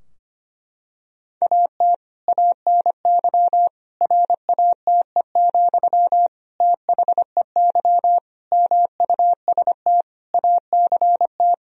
Morse code, heard as a keyed tone, about 25 wpm, 700 Hz. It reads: AT ANY RATE, THEY MUST ACT
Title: Sentences from Top 1000 Words - Review Part 4 - 25wpm